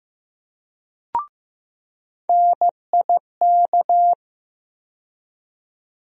Morse code, heard as a keyed tone, about 15 wpm, 700 Hz. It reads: NIK
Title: USA Call Sign Postfixes - 3 Letters 15wpm